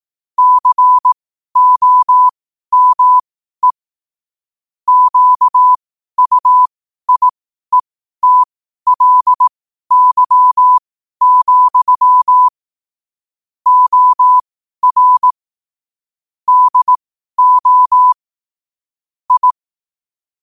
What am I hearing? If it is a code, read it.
Morse code, heard as a keyed tone, about 18 wpm, spaced as for 12 wpm, 1000 Hz. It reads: COME QUIETLY, OR DO I
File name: qotdfiles / 12wpm